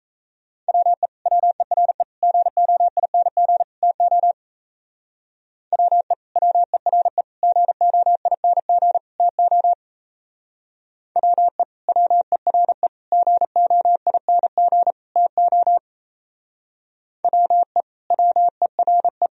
Morse code, heard as a keyed tone, about 36 wpm, 700 Hz. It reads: WE WERE GOING TO WE WERE GOING TO WE WERE GOING TO WE WERE